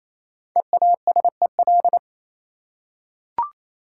Morse code, heard as a keyed tone, about 27 wpm, 700 Hz. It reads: EASEL
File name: Easy Words - Speed-Racing with Hints - 22wpm